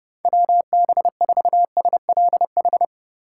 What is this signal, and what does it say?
Morse code, heard as a keyed tone, about 31 wpm, 700 Hz. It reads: WB4SLH